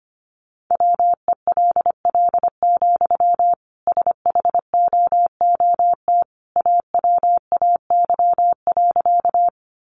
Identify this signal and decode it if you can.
Morse code, heard as a keyed tone, about 25 wpm, 700 Hz. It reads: WELL, SHOOT AWAY.
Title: Sentences from Top 1200 Words 25wpm